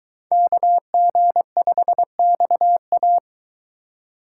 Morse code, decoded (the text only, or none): KG5XA